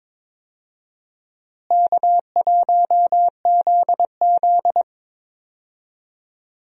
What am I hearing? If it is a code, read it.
Morse code, heard as a keyed tone, about 21 wpm, 700 Hz. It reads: K1ZZ
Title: USA Call Signs - 4 Characters - Repeated 2x - 22wpm